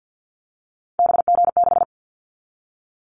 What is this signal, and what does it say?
Morse code, heard as a keyed tone, about 50 wpm, 700 Hz. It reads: BG6
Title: Callsign Prefixes for Top 100 DXCC - 50wpm